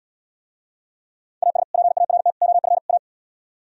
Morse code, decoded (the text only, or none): AS PART OF A